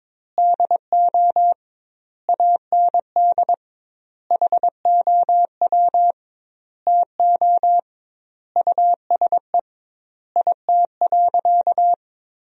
Morse code, heard as a keyed tone, about 22 wpm, 700 Hz. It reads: DO AND HOW TO USE IT.
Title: 2x Word-Spacing - Sentences from Top 500 Words - 22wpm